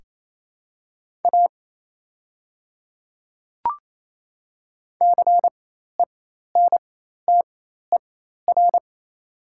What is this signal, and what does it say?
Morse code, heard as a keyed tone, about 28 wpm, spaced as for 13 wpm, 700 Hz. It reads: A CENTER
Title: 4x Character Spacing - Top 500 Words - 28wpm